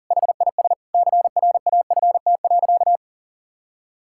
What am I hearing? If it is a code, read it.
Morse code, heard as a keyed tone, about 40 wpm, 700 Hz. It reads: HIS CRAFT.
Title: Sentences from Top 1000 Words - Review Part 6 - 40wpm